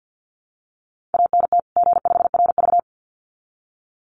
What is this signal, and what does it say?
Morse code, heard as a keyed tone, about 50 wpm, 700 Hz. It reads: ANT G5RV
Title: Common QSO Phrases 50wpm